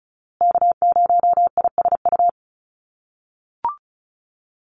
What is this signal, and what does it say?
Morse code, heard as a keyed tone, about 34 wpm, 700 Hz. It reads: K0ISU
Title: USA Call Signs Repeated 2x 35wpm